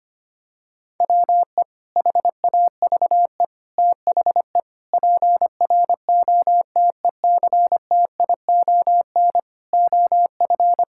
Morse code, heard as a keyed tone, about 25 wpm, 700 Hz. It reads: WE HAVE THE PROTECTION OF
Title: Sentences from Top 1600 Words 25wpm